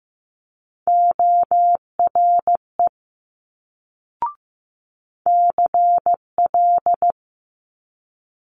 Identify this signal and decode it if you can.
Morse code, heard as a keyed tone, about 15 wpm, 700 Hz. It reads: ORE CL
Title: Multi-letter Phonemes Quick Fire 15wpm